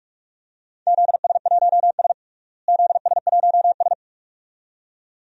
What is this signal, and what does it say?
Morse code, heard as a keyed tone, about 43 wpm, 700 Hz. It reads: ZS1S ZS1S
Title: DXCC-100 Call Signs - 4 Characters - Repeated 2x - 45wpm